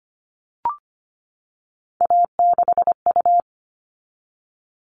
Morse code, heard as a keyed tone, about 25 wpm, 700 Hz. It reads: A6U
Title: All Cut Numbers - 3-Digit 25wpm